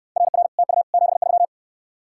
Morse code, heard as a keyed tone, about 69 wpm, 700 Hz. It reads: RR TU 73